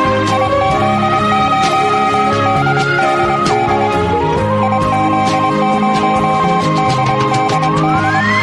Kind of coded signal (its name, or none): Morse code